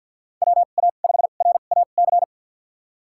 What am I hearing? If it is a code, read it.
Morse code, heard as a keyed tone, about 52 wpm, spaced as for 37 wpm, 700 Hz. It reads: WA5RAC
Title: USA Call Signs - 2x Character Spacing - 50wpm